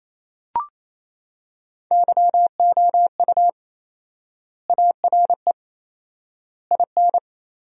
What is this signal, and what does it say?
Morse code, heard as a keyed tone, about 28 wpm, 700 Hz. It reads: YOU ARE IN
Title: 4x Word-Spacing - Sentences from Top 400 Words - 28wpm